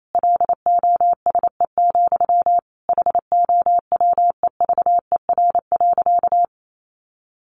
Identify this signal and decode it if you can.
Morse code, heard as a keyed tone, about 28 wpm, 700 Hz. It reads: LOSE, HOWEVER.
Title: Sentences from Top 1100 Words 28wpm